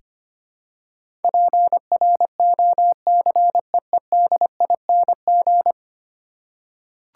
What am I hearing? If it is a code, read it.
Morse code, heard as a keyed tone, about 25 wpm, 700 Hz. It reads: PROCEEDING
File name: 10 Letter Words - First 3 Letters Given - 25wpm